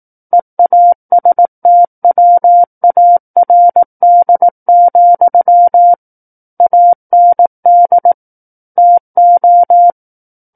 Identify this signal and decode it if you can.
Morse code, heard as a keyed tone, about 18 wpm, 700 Hz. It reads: EASTWARD, AND TO
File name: Book One / 09-Chapter 9 - 18 WPM